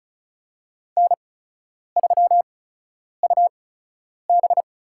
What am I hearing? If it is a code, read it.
Morse code, heard as a keyed tone, about 36 wpm, spaced as for 10 wpm, 700 Hz. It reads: N3UB
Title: USA Call Signs - 4 Characters - 8x Character Spacing - 35wpm